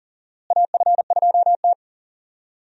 Morse code, heard as a keyed tone, about 39 wpm, 700 Hz. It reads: AF2T
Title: USA Call Signs - 4 Characters - Repeated 3x - 40wpm